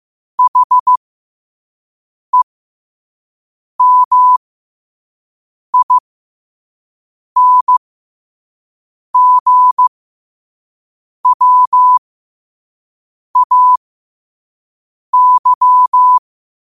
Morse code, heard as a keyed tone, about 15 wpm, spaced as for 5 wpm, 1000 Hz. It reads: HEMINGWAY
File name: qotdfiles / 5wpm